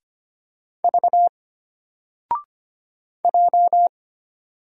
Morse code, heard as a keyed tone, about 25 wpm, 700 Hz. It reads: V J
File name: Single-Letter-Number-25wpm